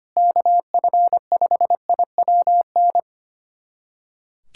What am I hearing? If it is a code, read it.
Morse code, heard as a keyed tone, about 25 wpm, 700 Hz. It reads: KF5IWN
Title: USA Callsign DE Callsign - 6 Characters - 25wpm